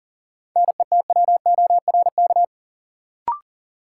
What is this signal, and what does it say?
Morse code, heard as a keyed tone, about 40 wpm, 700 Hz. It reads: NETWORK
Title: Sentences from Top 1000 Words - Review Part 7 - New Word Drill - 40wpm